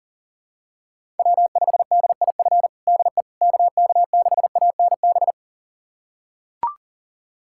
Morse code, heard as a keyed tone, about 37 wpm, 700 Hz. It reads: W5DIF DE KK6ANB
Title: USA Callsign DE Callsign 40wpm